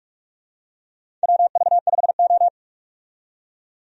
Morse code, heard as a keyed tone, about 46 wpm, 700 Hz. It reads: WV5O
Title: USA Call Signs - 4 Characters - 45wpm